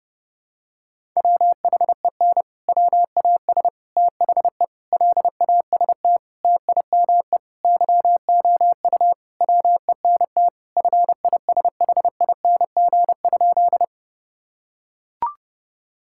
Morse code, heard as a keyed tone, about 30 wpm, 700 Hz. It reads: WHEN WAS THE LAST TIME YOU WENT FISHING?
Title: Sentences from Top 1000 Words - Review Part 5 - 30wpm